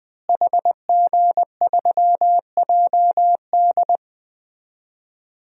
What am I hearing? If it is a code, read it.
Morse code, heard as a keyed tone, about 21 wpm, 700 Hz. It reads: HG3JD
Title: DXCC-100 Call Signs - 5 Characters - Repeated 3x - 20wpm